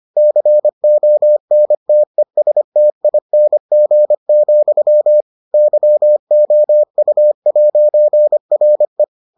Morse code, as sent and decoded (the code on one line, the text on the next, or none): -.-. --- -. - . ... - .. -. --. --..-- / -.-- --- ..- .----. .-. .
CONTESTING, YOU'RE